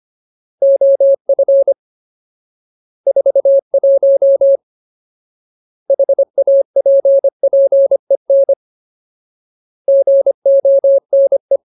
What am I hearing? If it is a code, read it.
Morse code, heard as a keyed tone, about 25 wpm, 550 Hz. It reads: OF 41 HAPPEN GONE
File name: Ultimate Word List / 25WPM Ultimate